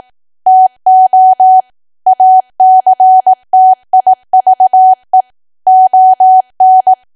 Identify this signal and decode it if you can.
Morse code, heard as a keyed tone, about 18 wpm, 750 Hz. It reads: TO ACTIVE ON
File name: cwa / W1AW-Level2 / 18WPM